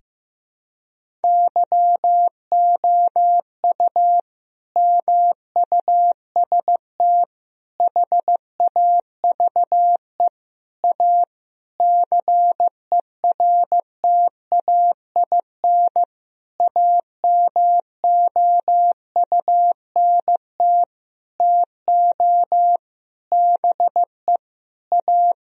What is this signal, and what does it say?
Morse code, heard as a keyed tone, about 16 wpm, 700 Hz. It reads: YOU MUST HAVE A CERTAIN AMOUNT TO BE A
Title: Sentences from Top 1000 Words - Review Part 4 - 15wpm